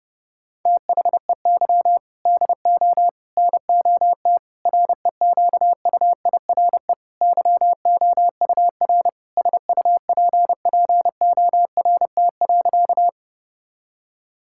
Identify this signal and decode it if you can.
Morse code, heard as a keyed tone, about 30 wpm, 700 Hz. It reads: THEY DO NOT REQUIRE YOUR SUPPORT.
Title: Sentences from Top 1000 Words - Review Part 6 - 30wpm